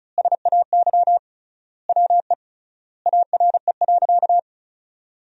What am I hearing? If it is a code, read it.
Morse code, heard as a keyed tone, about 36 wpm, 700 Hz. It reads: SAY WE ARE.
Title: 3x Word-Spacing - Sentences from Top 200 Words - 35wpm